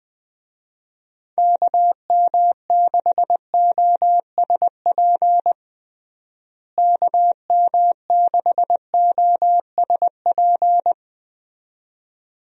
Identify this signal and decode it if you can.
Morse code, heard as a keyed tone, about 21 wpm, 700 Hz. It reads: KM6OSP KM6OSP
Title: USA Call Signs - 6 Characters - Repeated 2x - 20wpm